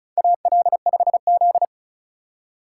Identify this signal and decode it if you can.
Morse code, heard as a keyed tone, about 34 wpm, 700 Hz. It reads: AL5Z
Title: USA Call Signs - 4 Characters - 2x Character Spacing - 35wpm